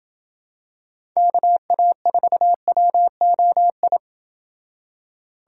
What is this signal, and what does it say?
Morse code, heard as a keyed tone, about 28 wpm, 700 Hz. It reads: KA4WOI